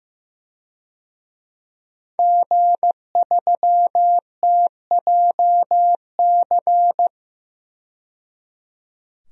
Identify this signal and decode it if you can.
Morse code, heard as a keyed tone, about 15 wpm, 700 Hz. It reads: G3TJC
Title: DXCC-100 Call Signs - 5 Characters - 15wpm